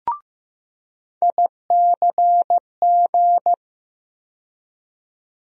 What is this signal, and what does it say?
Morse code, heard as a keyed tone, about 15 wpm, 700 Hz. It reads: ICG
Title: USA Call Sign Postfixes - 3 Letters 15wpm